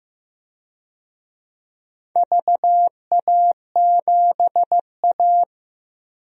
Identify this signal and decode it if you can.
Morse code, heard as a keyed tone, about 15 wpm, 700 Hz. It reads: VA7A
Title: DXCC-100 Call Signs - 4 Characters - 15wpm